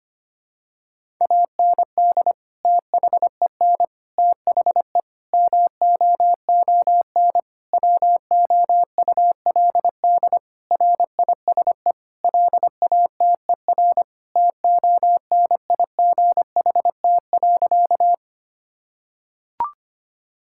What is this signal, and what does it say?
Morse code, heard as a keyed tone, about 25 wpm, 700 Hz. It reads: AND THEN THE MOON WOULD RISE LATER TONIGHT.
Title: Sentences from Top 1000 Words - Review Part 6 - 25wpm